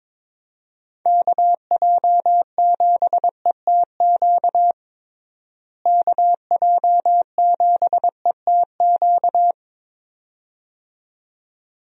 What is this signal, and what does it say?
Morse code, heard as a keyed tone, about 22 wpm, 700 Hz. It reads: KJ7ETQ KJ7ETQ
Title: USA Call Signs - 6 Characters - Repeated 2x - 22wpm